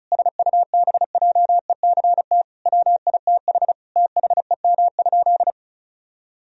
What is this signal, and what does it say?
Morse code, heard as a keyed tone, about 36 wpm, 700 Hz. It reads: SUBJECT WITH THEM?
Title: Sentences from Top 1000 Words - Review Part 1 - 35wpm